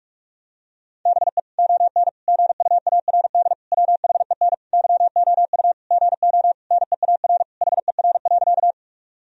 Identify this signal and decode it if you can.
Morse code, heard as a keyed tone, about 41 wpm, 700 Hz. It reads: BE ON GUARD WHEN YOU GO NEAR HER.
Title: Sentences from Top 1000 Words - Review Part 4 - 45wpm